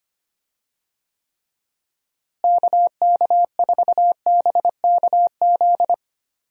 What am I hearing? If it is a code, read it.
Morse code, heard as a keyed tone, about 25 wpm, 700 Hz. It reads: KK4BKZ
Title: USA Call Signs - 6 Characters - Repeated 3x - 25wpm